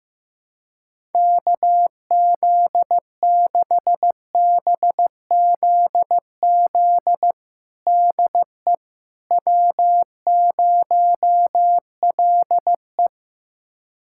Morse code, heard as a keyed tone, about 15 wpm, 700 Hz. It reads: KZ6BZZ DE W0LE
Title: USA Callsign DE Callsign 15wpm